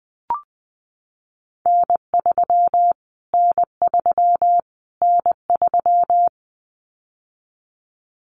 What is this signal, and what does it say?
Morse code, heard as a keyed tone, about 20 wpm, 700 Hz. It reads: N3 N3 N3
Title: Callsign Prefixes for Top 25 DXCC - Repeated 3x - 20wpm